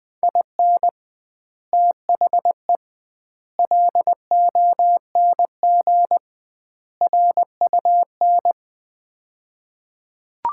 Morse code, heard as a keyed tone, about 20 wpm, 700 Hz. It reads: IN THE LONG RUN
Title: 2x Word-Spacing - Sets of 4 Words - 20wpm